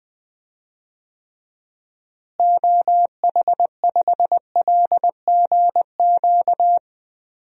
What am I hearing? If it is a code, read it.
Morse code, heard as a keyed tone, about 20 wpm, 700 Hz. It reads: OH5LGQ
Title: DXCC-100 Call Signs - 6 Characters - Repeated 2x - 20wpm